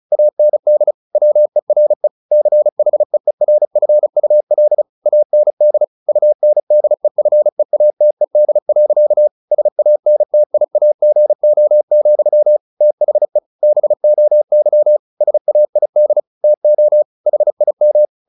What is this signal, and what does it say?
Morse code, heard as a keyed tone, about 34 wpm, 600 Hz. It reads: AND WERE CHEERFUL AND UNDEFEATED. SANTIAGO, THE BOY SAID TO HIM
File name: the old man and the sea / the old man and the sea 1 35wpm